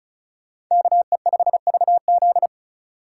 Morse code, heard as a keyed tone, about 39 wpm, 700 Hz. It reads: KE5VZ